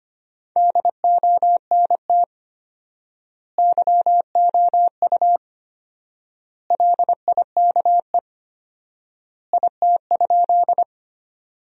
Morse code, heard as a keyed tone, about 25 wpm, 700 Hz. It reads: DONT YOU LIKE IT?
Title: 4x Word-Spacing - Sentences from Top 300 Words - 25wpm